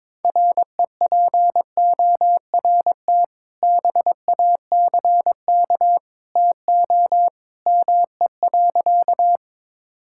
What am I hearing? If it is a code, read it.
Morse code, heard as a keyed tone, about 22 wpm, 700 Hz. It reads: REPORT BACK TO ME.
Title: Sentences from Top 1000 Words - Review Part 6 - 22wpm